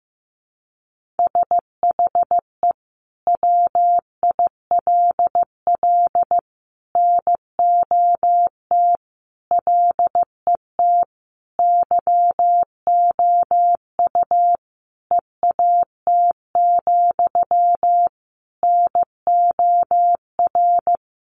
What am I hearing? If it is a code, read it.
Morse code, heard as a keyed tone, about 15 wpm, 700 Hz. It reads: SHE WILL NOT LET YOU EAT, NOR
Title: Sentences from Top 1500 Words 15wpm